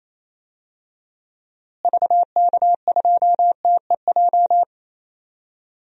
Morse code, heard as a keyed tone, about 28 wpm, 700 Hz. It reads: VK2TEJ
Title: DXCC-100 Call Signs - 6 Characters - Repeated 2x - 28wpm